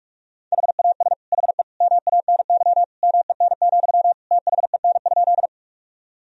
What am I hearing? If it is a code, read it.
Morse code, heard as a keyed tone, about 42 wpm, 700 Hz. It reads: HAS HE MANY MEN, THEN?
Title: Sentences from Top 1000 Words - Review Part 1 - 45wpm